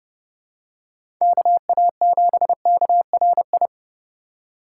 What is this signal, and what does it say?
Morse code, heard as a keyed tone, about 28 wpm, 700 Hz. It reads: KA7KRI